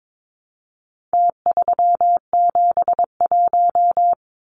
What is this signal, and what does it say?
Morse code, heard as a keyed tone, about 22 wpm, 700 Hz. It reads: T371
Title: Numbers - Contest Style - 4-Digit 22wpm